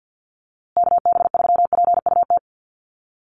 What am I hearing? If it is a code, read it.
Morse code, heard as a keyed tone, about 49 wpm, 700 Hz. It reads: KB3PUT
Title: USA Call Signs Repeated 3x 50wpm